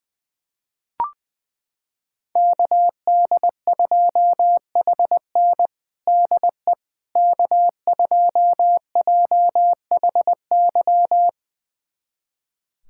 Morse code, heard as a keyed tone, about 20 wpm, 700 Hz. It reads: KD2HN DE K2JHY
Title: USA Callsign DE Callsign - 5 Characters - 20wpm